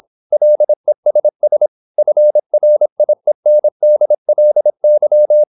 Morse code, heard as a keyed tone, about 26 wpm, 600 Hz. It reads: LESS FRIENDLY
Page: Never 26wpm | CW med Gnister